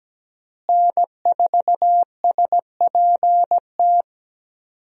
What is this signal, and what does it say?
Morse code, heard as a keyed tone, about 17 wpm, 700 Hz. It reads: N4SPT
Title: USA Call Signs - 5 Characters - Repeated 2x - 17wpm